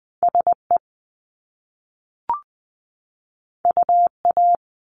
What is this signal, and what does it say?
Morse code, heard as a keyed tone, about 20 wpm, 700 Hz. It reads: SE UA
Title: Top Two Letter Combinations in Words 20wpm